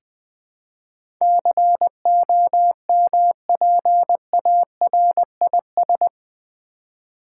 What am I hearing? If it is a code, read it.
Morse code, heard as a keyed tone, about 20 wpm, 700 Hz. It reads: COMPARIS
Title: Build A Word - 4 Letters At A Time - 10-Letter Words - 20wpm